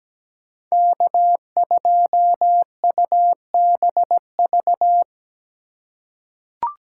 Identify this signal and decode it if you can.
Morse code, heard as a keyed tone, about 17 wpm, 700 Hz. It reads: K2UBV
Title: USA Call Signs - 5 Characters - 4x Character Spacing - 17wpm